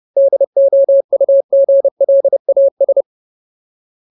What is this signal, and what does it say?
Morse code, heard as a keyed tone, about 30 wpm, 550 Hz. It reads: DOUGLAS
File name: Ultimate Word List / 30WPM Ultimate